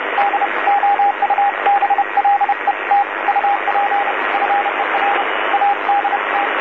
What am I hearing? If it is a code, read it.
Morse code, heard as a keyed tone, about 30 wpm, 810 Hz, with radio noise. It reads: DOUBLET UP FER ADT
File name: DC-TRCVR / 80-CW